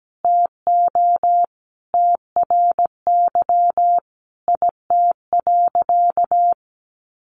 Morse code, heard as a keyed tone, about 17 wpm, 700 Hz. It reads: TO TRY IT.
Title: Sentences from Top 1100 Words 17wpm